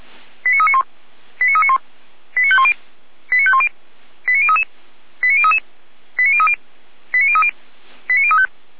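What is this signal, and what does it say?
Five-tone selective call (ZVEI), heard as quick runs of five tones, tones of about 70 ms. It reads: 89391 89391 89510 89510 89033 89033 89030 89030 89035
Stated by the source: ZVEI five-tone selective call received on an SDR.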